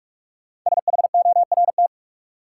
Morse code, heard as a keyed tone, about 42 wpm, 700 Hz. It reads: SHORT
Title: Sets of 5 Words Encore - New Word Drill - 45wpm